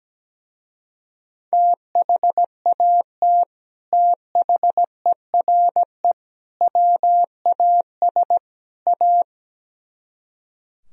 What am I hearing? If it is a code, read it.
Morse code, heard as a keyed tone, about 17 wpm, 700 Hz. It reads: THAT THERE WAS A
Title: Speed-Racing - Sets of 4 Words 17wpm